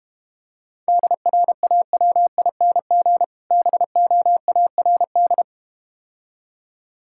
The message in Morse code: DRAWING BOARD

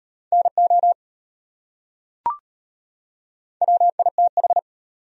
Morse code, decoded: NO WITH

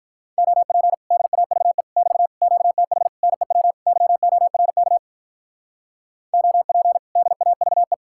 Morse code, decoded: OP DAVE <BT> QTH NEW YORK OP DAVE

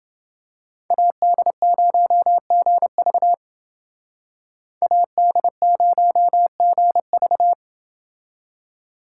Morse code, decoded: AD0GV AD0GV

